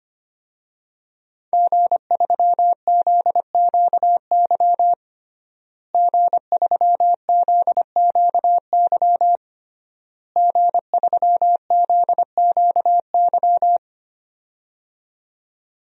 Morse code, heard as a keyed tone, about 25 wpm, 700 Hz. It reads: G3ZQY G3ZQY G3ZQY
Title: DXCC-100 Call Signs - 5 Characters - Repeated 3x - 25wpm